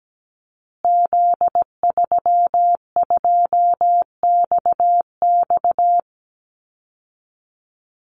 Morse code, heard as a keyed tone, about 17 wpm, 700 Hz. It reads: Z32XX
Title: Call Signs Repeated 2x - 17wpm